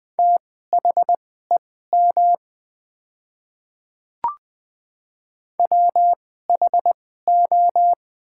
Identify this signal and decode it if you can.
Morse code, heard as a keyed tone, about 20 wpm, spaced as for 14 wpm, 700 Hz. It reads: THEM WHO
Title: 2x Character Spacing - Top 100 Words - 20wpm